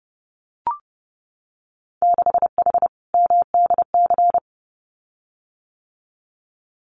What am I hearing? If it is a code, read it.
Morse code, heard as a keyed tone, about 30 wpm, 700 Hz. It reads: 6H MDC
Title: Winter Field Day Exchanges Encore 30wpm